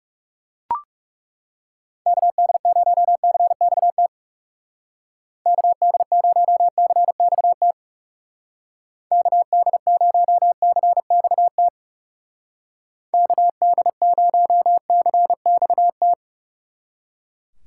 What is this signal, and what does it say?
Morse code, heard as a keyed tone, about 36 wpm, 700 Hz. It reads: KD0CXT KD0CXT KD0CXT KD0CXT
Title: Speed Racing - USA 2x3 Call Signs 30wpm